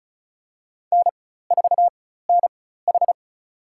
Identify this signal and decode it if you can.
Morse code, heard as a keyed tone, about 34 wpm, spaced as for 16 wpm, 700 Hz. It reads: N4NH
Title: USA Call Signs - 4 Characters - 4x Character Spacing - 35wpm